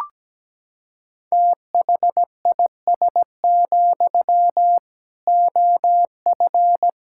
Morse code, heard as a keyed tone, about 17 wpm, 700 Hz. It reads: THIS, OF
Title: Sentences from Top 1600 Words 17wpm